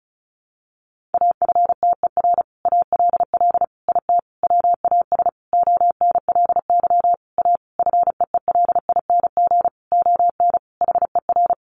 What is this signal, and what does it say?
Morse code, heard as a keyed tone, about 34 wpm, 700 Hz. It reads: AFTER ALL IT WAS ONLY A FEELING ON HER
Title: Sentences from Top 1200 Words 35wpm